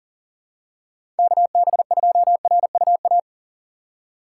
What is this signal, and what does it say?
Morse code, heard as a keyed tone, about 38 wpm, 700 Hz. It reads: KB2RUA
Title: USA Call Signs - 2x Character Spacing - 40wpm